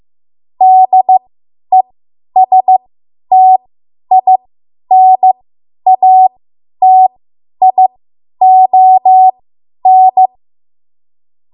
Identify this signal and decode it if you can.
Morse code, heard as a keyed tone, about 15 wpm, spaced as for 10 wpm, 750 Hz. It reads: DESTINATION
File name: Morse / 10WPM